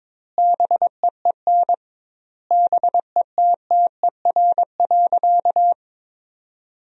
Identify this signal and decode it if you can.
Morse code, heard as a keyed tone, about 22 wpm, 700 Hz. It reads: BEEN BETTER.
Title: 2x Word-Spacing - Sentences from Top 500 Words - 22wpm